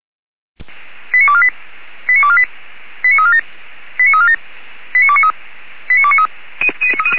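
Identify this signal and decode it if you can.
Five-tone selective call (ZVEI), heard as quick runs of five tones, tones of about 70 ms. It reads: 89248 89248 89347 89347 89293 89293
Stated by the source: ZVEI five-tone call received by SDR.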